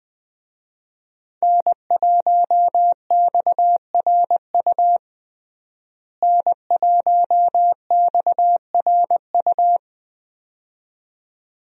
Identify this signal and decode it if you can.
Morse code, heard as a keyed tone, about 20 wpm, 700 Hz. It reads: N1XRU N1XRU
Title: USA Call Signs - 5 Characters - Repeated 2x - 20wpm